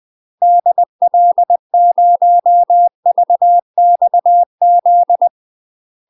Morse code, heard as a keyed tone, about 20 wpm, 700 Hz. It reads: DL0VXZ
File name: UK MASTER QSOs TRAINING / 20WPM / UK MASTER QSOs 0018